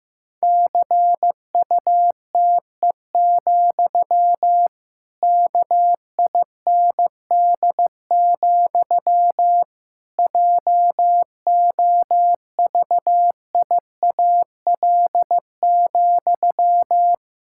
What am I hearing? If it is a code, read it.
Morse code, heard as a keyed tone, about 15 wpm, 700 Hz. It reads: CUTE, KIND, JOVIAL,
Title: Pangram Challenge v2 - 15wpm